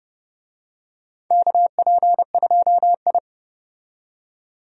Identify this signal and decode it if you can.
Morse code, heard as a keyed tone, about 30 wpm, 700 Hz. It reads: KP2I
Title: DXCC-100 Call Signs - 4 Characters - 30wpm